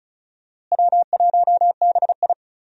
Morse code, heard as a keyed tone, about 34 wpm, 700 Hz. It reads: W1BI
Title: USA Call Signs - 4 Characters - Repeated 3x - 35wpm